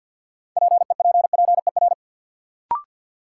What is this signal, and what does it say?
Morse code, heard as a keyed tone, about 51 wpm, 700 Hz. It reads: PEPPER
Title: Sets of 5 Words - New Word Drill - 50wpm